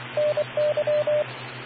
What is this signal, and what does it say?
Morse code, heard as a keyed tone, about 25 wpm, 600 Hz, with radio noise. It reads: NY